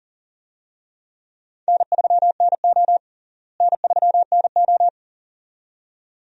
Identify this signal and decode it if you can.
Morse code, heard as a keyed tone, about 40 wpm, 700 Hz. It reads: N3NO N3NO
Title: USA Call Signs - 4 Characters - Repeated 2x - 40wpm